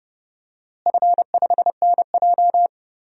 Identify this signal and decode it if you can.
Morse code, heard as a keyed tone, about 30 wpm, 700 Hz. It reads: F5NJ
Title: DXCC-100 Call Signs - 4 Characters - Repeated 3x - 30wpm